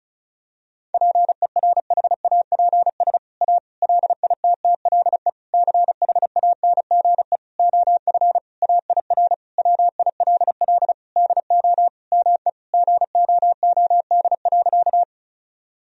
Morse code, heard as a keyed tone, about 36 wpm, 700 Hz. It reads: PERHAPS A LITTLE CHANGE OF AIR WILL DO ME GOOD.
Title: Sentences from Top 1000 Words - Review Part 1 - 35wpm